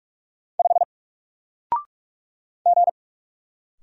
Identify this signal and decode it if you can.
Morse code, heard as a keyed tone, about 44 wpm, 700 Hz. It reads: 5 G
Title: Single Character (Letter:Number:Punctuation:Prosign) Alt 45wpm